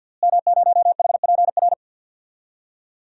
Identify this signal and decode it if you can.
Morse code, heard as a keyed tone, about 49 wpm, 700 Hz. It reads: M0HPR
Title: DXCC-100 Call Signs - 5 Characters - Repeated 2x - 50wpm